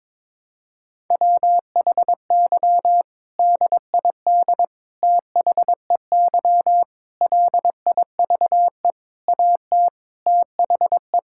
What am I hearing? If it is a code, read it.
Morse code, heard as a keyed tone, about 22 wpm, 700 Hz. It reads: WHY DID THEY LIVE AT THE
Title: Sentences from Top 1000 Words - Review Part 6 - 22wpm